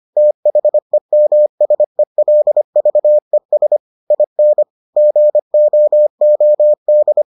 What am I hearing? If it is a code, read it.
Morse code, heard as a keyed tone, about 25 wpm, 600 Hz. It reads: THEMSELVES IN GOOD